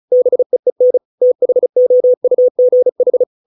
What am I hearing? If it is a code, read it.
Morse code, heard as a keyed tone, about 34 wpm, 500 Hz. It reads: BEEN THOUGH